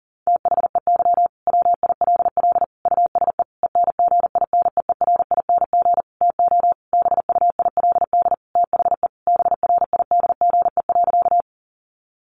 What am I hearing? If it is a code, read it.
Morse code, heard as a keyed tone, about 40 wpm, 700 Hz. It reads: THEY WILL USE ENGINEERING TO BUILD THE BRIDGE.
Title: Sentences from Top 1500 Words 40wpm